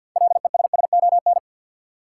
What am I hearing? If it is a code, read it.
Morse code, heard as a keyed tone, about 52 wpm, 700 Hz. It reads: LESSON
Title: Six Letter Words - 50wpm